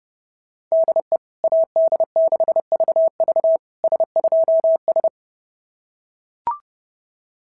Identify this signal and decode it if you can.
Morse code, heard as a keyed tone, about 28 wpm, 650 Hz. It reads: DE AD6VV S2S
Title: SOTA QSO Practice 30wpm